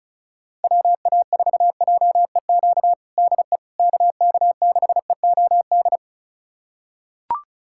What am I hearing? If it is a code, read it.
Morse code, heard as a keyed tone, about 36 wpm, 700 Hz. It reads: WA4JEQ DE KK6EOD
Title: USA Callsign DE Callsign 35wpm